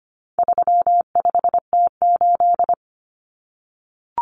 Morse code, heard as a keyed tone, about 25 wpm, 700 Hz. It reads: 35T8
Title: Common Cut Numbers - 4-Digit 25wpm